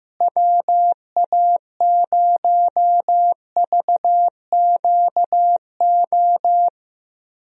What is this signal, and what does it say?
Morse code, heard as a keyed tone, about 15 wpm, 700 Hz. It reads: WA0VQO
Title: USA Call Signs - 6 Characters - Repeated 3x - 15wpm